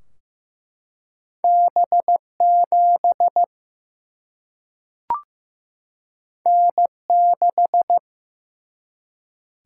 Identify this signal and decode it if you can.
Morse code, heard as a keyed tone, about 15 wpm, 700 Hz. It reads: B7 N6
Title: Single-Letter Single-Number 15wpm